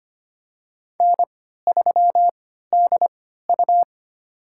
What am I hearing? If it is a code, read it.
Morse code, heard as a keyed tone, about 25 wpm, spaced as for 14 wpm, 700 Hz. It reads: N3DU
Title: USA Call Signs - 4 Characters - 3x Character Spacing - 25wpm